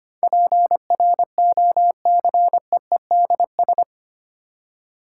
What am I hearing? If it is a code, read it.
Morse code, heard as a keyed tone, about 25 wpm, 700 Hz. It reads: PROCEEDS
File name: 8 Letter Words - First 3 Letters Given - 25wpm